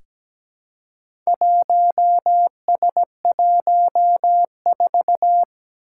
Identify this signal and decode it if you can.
Morse code, heard as a keyed tone, about 17 wpm, 700 Hz. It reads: 1S14
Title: Four Challenging Letter-Number Combinations 17wpm